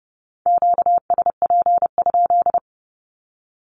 Morse code, heard as a keyed tone, about 30 wpm, 700 Hz. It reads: QSP?
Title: Q-Codes Repeated 3x 30wpm